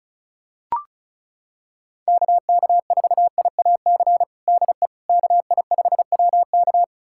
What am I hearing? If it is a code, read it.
Morse code, heard as a keyed tone, about 34 wpm, 700 Hz. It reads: KK4IAC DE KI5WK
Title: USA Callsign DE Callsign 35wpm